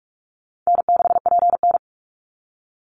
Morse code, heard as a keyed tone, about 44 wpm, 700 Hz. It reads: N6PN
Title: Easy Call Signs - 45wpm